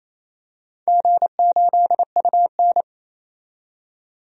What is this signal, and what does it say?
Morse code, heard as a keyed tone, about 28 wpm, 700 Hz. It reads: G8UN
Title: DXCC-100 Call Signs - 4 Characters - 28wpm